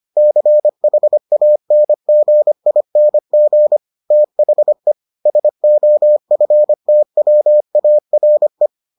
Morse code, heard as a keyed tone, about 25 wpm, 600 Hz. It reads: CHANGING THE SOFTWARE